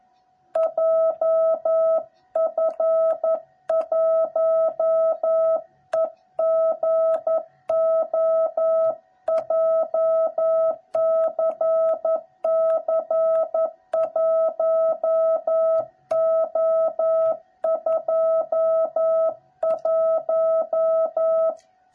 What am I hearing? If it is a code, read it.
Morse code, heard as a keyed tone, about 11 wpm, 660 Hz. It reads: JF1EGO JCC1O21
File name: jh8cbh / 2021cw / cw06b